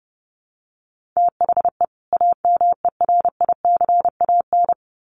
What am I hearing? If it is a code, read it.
Morse code, heard as a keyed tone, about 30 wpm, 700 Hz. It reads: THE AMERICAN
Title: Sets of 2 Words 30wpm